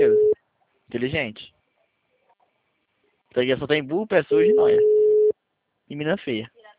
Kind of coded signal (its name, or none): telephone call progress tone